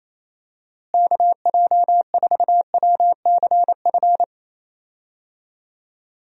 Morse code, heard as a keyed tone, about 28 wpm, 700 Hz. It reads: KJ4WCF
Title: USA Call Signs - 6 Characters - Repeated 3x - 28wpm